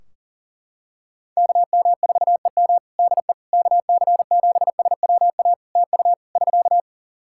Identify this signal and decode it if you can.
Morse code, heard as a keyed tone, about 40 wpm, 700 Hz. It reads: KM4EM DE KC7SWA TU <SK>
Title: Practice QSO - Exchange 4 - Ending - 40wpm